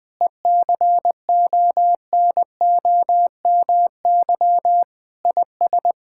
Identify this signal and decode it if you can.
Morse code, heard as a keyed tone, about 21 wpm, 700 Hz. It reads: ECONOMY IS
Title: Sentences from Top 1600 Words 20wpm